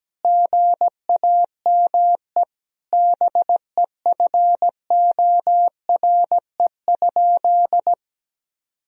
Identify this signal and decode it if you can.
Morse code, heard as a keyed tone, about 17 wpm, 700 Hz. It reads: GAME BEFORE?
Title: Sentences from Top 1000 Words - Review Part 3 - 17wpm